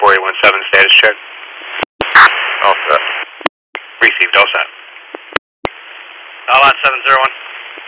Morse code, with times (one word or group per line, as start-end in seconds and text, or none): none